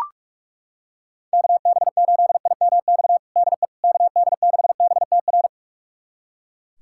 Morse code, heard as a keyed tone, about 43 wpm, 700 Hz. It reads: KB8IMX DE KD6BTR
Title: USA Callsign DE Callsign 45wpm